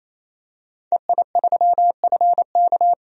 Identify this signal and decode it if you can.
Morse code, heard as a keyed tone, about 28 wpm, 700 Hz. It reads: EI3FK
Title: DXCC-100 Call Signs - 5 Characters - 28wpm